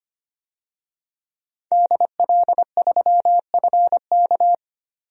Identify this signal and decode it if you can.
Morse code, heard as a keyed tone, about 25 wpm, 700 Hz. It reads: DL3FK
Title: DXCC-100 Call Signs - 5 Characters - 25wpm